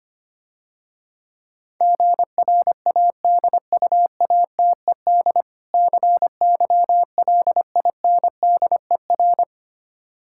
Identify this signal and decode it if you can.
Morse code, heard as a keyed tone, about 26 wpm, 700 Hz. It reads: GRADUATED CYLINDER
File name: Easy Words - Speed-Racing with Hints - 25wpm